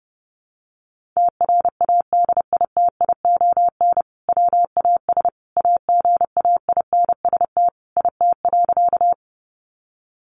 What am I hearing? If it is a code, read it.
Morse code, heard as a keyed tone, about 28 wpm, 700 Hz. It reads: TRADITION WAS AGAINST IT.
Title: Sentences from Top 1500 Words 30wpm